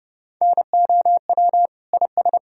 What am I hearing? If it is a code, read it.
Morse code, heard as a keyed tone, about 30 wpm, 700 Hz. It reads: NOW IS
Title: Pangram Challenge v2 - 30wpm